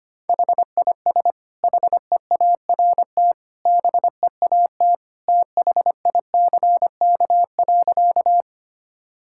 Morse code, heard as a keyed tone, about 25 wpm, 700 Hz. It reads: HIS HEART BEAT THICK.
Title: Sentences from Top 1600 Words 25wpm